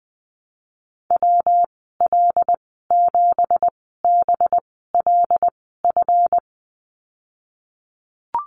Morse code, heard as a keyed tone, about 19 wpm, spaced as for 14 wpm, 700 Hz. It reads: WL7BLF
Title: 186 Lesson 30g - Letter b - Call Signs 20wpm